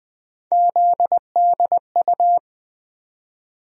Morse code, heard as a keyed tone, about 21 wpm, 700 Hz. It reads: ZDU
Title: USA Call Sign Postfixes - 3 Letters 20wpm